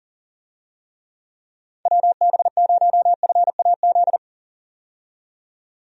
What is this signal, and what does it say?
Morse code, heard as a keyed tone, about 40 wpm, 700 Hz. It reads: WB0FAZ
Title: USA Call Signs - 6 Characters - 40wpm